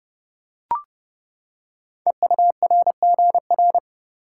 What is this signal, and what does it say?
Morse code, heard as a keyed tone, about 30 wpm, 700 Hz. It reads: EURGR
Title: Code Groups - 5 Letters - 30wpm